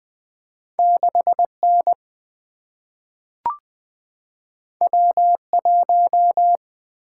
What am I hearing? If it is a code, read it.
Morse code, heard as a keyed tone, about 20 wpm, 690 Hz. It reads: 6N W1
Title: Two Letter-Number Combinations - 20wpm